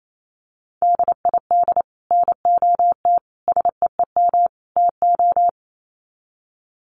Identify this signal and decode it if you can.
Morse code, heard as a keyed tone, about 28 wpm, 700 Hz. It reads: DID NOT SEEM TO
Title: Sets of 4 Words 28wpm